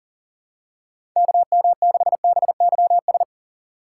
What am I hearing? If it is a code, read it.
Morse code, heard as a keyed tone, about 40 wpm, 700 Hz. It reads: KM6BYS
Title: USA Call Signs - 6 Characters - 2x Character Spacing - 40wpm